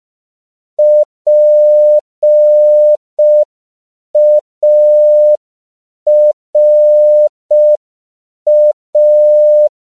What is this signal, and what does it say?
Morse code, heard as a keyed tone, about 5 wpm, 600 Hz. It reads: PARA
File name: CW / 5PPM